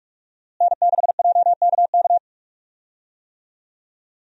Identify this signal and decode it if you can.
Morse code, heard as a keyed tone, about 44 wpm, 700 Hz. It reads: N6JKK
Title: USA Call Signs - 5 Characters - Repeated 3x - 45wpm